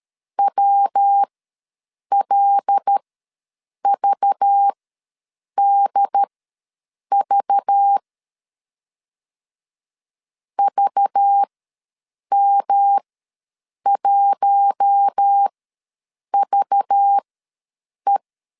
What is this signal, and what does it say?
Morse code, heard as a keyed tone, about 13 wpm, spaced as for 7 wpm, 800 Hz. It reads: WLVDV VM1VE
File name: LEZIONI / 5 LBVC / 5 LBV ws 12wpm 3015